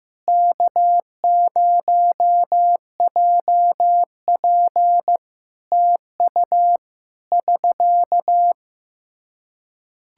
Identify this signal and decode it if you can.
Morse code, heard as a keyed tone, about 15 wpm, 700 Hz. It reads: K0JP TU <SK>
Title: Practice QSO - Exchange 4 - Ending - 15wpm